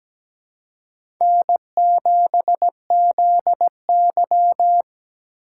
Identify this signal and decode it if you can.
Morse code, heard as a keyed tone, about 17 wpm, 700 Hz. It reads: N7ZY
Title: USA Call Signs - 4 Characters - 4x Character Spacing - 17wpm